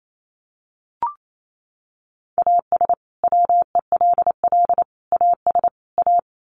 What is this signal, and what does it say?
Morse code, heard as a keyed tone, about 28 wpm, 700 Hz. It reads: AS WELL AS A
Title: Sets of 4 Words 28wpm